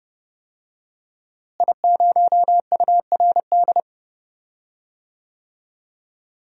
Morse code, heard as a keyed tone, about 30 wpm, 700 Hz. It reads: I0URD
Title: DXCC-100 Call Signs - 5 Characters - Repeated 3x - 30wpm